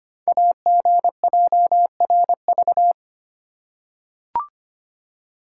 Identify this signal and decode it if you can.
Morse code, heard as a keyed tone, about 25 wpm, 700 Hz. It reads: AGJRV
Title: Code Groups - 5 Letter:Numbers - 25wpm